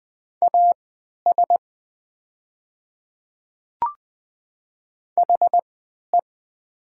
Morse code, heard as a keyed tone, about 20 wpm, spaced as for 11 wpm, 700 Hz. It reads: AS HE
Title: 3x Character Spacing - Top 100 Words - 20wpm